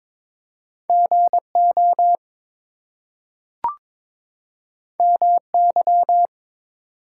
Morse code, heard as a keyed tone, about 22 wpm, 700 Hz. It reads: GO MY
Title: Two Letter Words - 22wpm